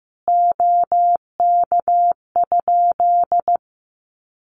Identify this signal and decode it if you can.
Morse code, heard as a keyed tone, about 15 wpm, 700 Hz. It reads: OK?
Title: Common QSO Phrases 15wpm